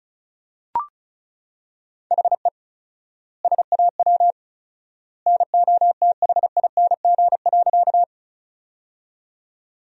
Morse code, heard as a keyed tone, about 34 wpm, 700 Hz. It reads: HE SAW NOTHING.
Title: 4x Word-Spacing - Sentences from Top 500 Words - 35wpm